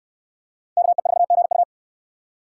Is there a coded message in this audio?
Yes: Morse code from 0.8 to 1.6 s, heard as a keyed tone, about 63 wpm, 700 Hz. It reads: KE4GU